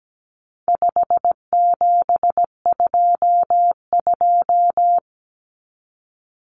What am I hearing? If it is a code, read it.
Morse code, heard as a keyed tone, about 17 wpm, 700 Hz. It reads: 5722
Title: Common Cut Numbers - 4-Digit 17wpm